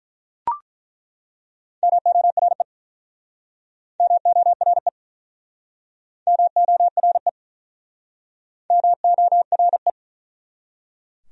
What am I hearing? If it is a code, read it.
Morse code, heard as a keyed tone, about 48 wpm, 700 Hz. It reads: MORE MORE MORE MORE